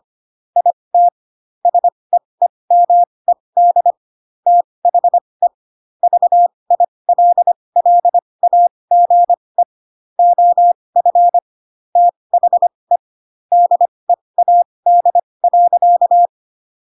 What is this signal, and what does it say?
Morse code, heard as a keyed tone, about 25 wpm, spaced as for 20 wpm, 700 Hz. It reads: IT SEEMED THE VILLAGE OF THE DEAD.
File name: Pinocchio / Pinocchio ch6 15 25 ik1hge